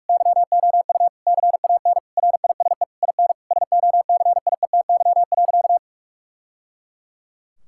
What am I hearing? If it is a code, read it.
Morse code, heard as a keyed tone, about 46 wpm, 700 Hz. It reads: YOU CAN RISE IN SOCIETY.
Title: Sentences from Top 1000 Words - Review Part 6 - 45wpm